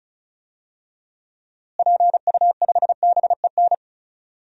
Morse code, heard as a keyed tone, about 34 wpm, 700 Hz. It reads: PU5BEN
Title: DXCC-100 Call Signs - 6 Characters - Repeated 3x - 35wpm